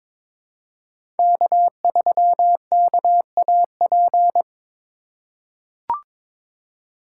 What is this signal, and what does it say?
Morse code, heard as a keyed tone, about 21 wpm, 700 Hz. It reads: K3KAP
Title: USA Call Signs - 5 Characters - 4x Character Spacing - 22wpm